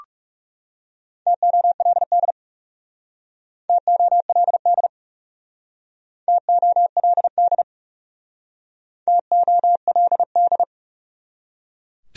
Morse code, heard as a keyed tone, about 38 wpm, 700 Hz. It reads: TOLD TOLD TOLD TOLD